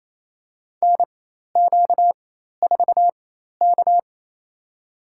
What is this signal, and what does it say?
Morse code, heard as a keyed tone, about 27 wpm, spaced as for 13 wpm, 700 Hz. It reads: NQ4K
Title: USA Call Signs - 4 Characters - 4x Character Spacing - 28wpm